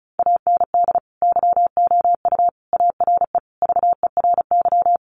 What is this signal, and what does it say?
Morse code, heard as a keyed tone, about 34 wpm, 700 Hz. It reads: AND YOU ARE VERY
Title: Sentences from Top 1100 Words 35wpm